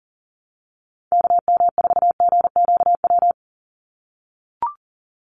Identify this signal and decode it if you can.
Morse code, heard as a keyed tone, about 40 wpm, 700 Hz. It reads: KM4GQW
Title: USA Call Signs 40wpm